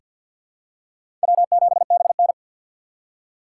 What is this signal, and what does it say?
Morse code, heard as a keyed tone, about 47 wpm, 700 Hz. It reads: W7BN
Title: USA Call Signs - 4 Characters - 6x Character Spacing - 50wpm